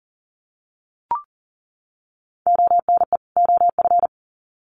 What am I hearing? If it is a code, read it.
Morse code, heard as a keyed tone, about 40 wpm, 700 Hz. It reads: ONE OF
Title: Sets of 2 Words 40wpm